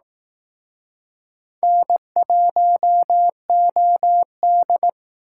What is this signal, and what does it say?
Morse code, heard as a keyed tone, about 19 wpm, 700 Hz. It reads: N1OD